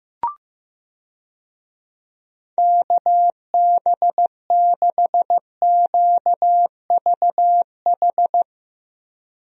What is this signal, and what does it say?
Morse code, heard as a keyed tone, about 15 wpm, 700 Hz. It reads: KB6QVH
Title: USA Call Signs - 6 Characters - Repeated 3x - 15wpm